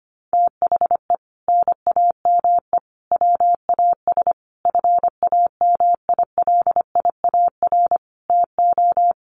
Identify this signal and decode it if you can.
Morse code, heard as a keyed tone, about 25 wpm, 700 Hz. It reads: THE NAME WAS FAMILIAR TO
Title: Sentences from Top 1000 Words 25wpm